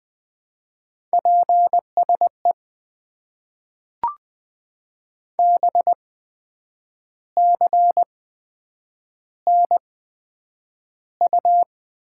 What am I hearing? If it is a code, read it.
Morse code, heard as a keyed tone, about 20 wpm, spaced as for 8 wpm, 700 Hz. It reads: PSE BCNU
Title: Top 90 QSO Elements - 8x Character Spacing - 20wpm